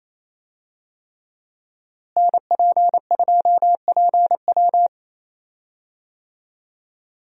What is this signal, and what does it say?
Morse code, heard as a keyed tone, about 28 wpm, 700 Hz. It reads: NP2PW
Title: USA Call Signs - 5 Characters - Repeated 3x - 28wpm